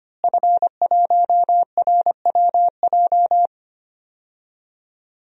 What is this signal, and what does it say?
Morse code, heard as a keyed tone, about 25 wpm, 700 Hz. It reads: F1RWJ
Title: DXCC-100 Call Signs - 5 Characters - Repeated 3x - 25wpm